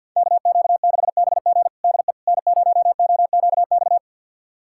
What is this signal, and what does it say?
Morse code, heard as a keyed tone, about 52 wpm, 690 Hz. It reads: KQ6BG DE N0OQX
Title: USA Callsign DE Callsign - 5 Characters - 50wpm